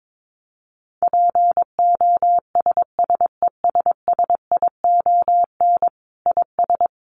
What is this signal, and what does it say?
Morse code, heard as a keyed tone, about 22 wpm, 700 Hz. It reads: POSSESSION IS
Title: Sentences from Top 1400 Words 22wpm